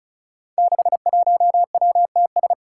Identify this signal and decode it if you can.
Morse code, heard as a keyed tone, about 36 wpm, 700 Hz. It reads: 61WTS
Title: Five Challenging Letter-Number Combinations 35wpm